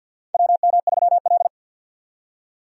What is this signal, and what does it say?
Morse code, heard as a keyed tone, about 49 wpm, 700 Hz. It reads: WM3L
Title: USA Call Signs - 4 Characters - 50wpm